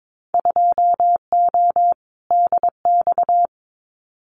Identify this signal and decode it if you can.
Morse code, heard as a keyed tone, about 22 wpm, 700 Hz. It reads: 2O DX
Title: Winter Field Day Exchanges Encore 22wpm